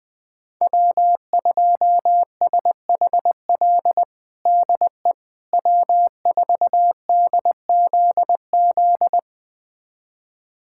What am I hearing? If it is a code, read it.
Morse code, heard as a keyed tone, about 20 wpm, 700 Hz. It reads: W2SHL DE W4DZZ
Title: USA Callsign DE Callsign - 5 Characters - 20wpm